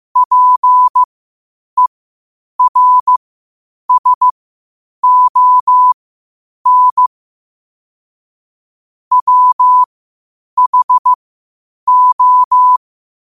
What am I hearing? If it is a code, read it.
Morse code, heard as a keyed tone, about 15 wpm, spaced as for 8 wpm, 1000 Hz. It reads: PERSON WHO
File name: qotdfiles / 8wpm